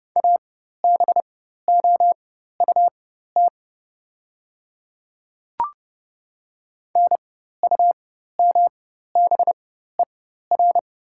Morse code, heard as a keyed tone, about 30 wpm, spaced as for 14 wpm, 700 Hz. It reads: ABOUT NUMBER
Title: 4x Character Spacing - Top 100 Words - 30wpm